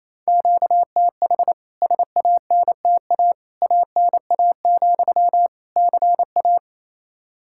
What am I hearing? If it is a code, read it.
Morse code, heard as a keyed tone, about 28 wpm, 700 Hz. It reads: QTH SANTA ANA, CA
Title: K1USN SST QSO Practice 28wpm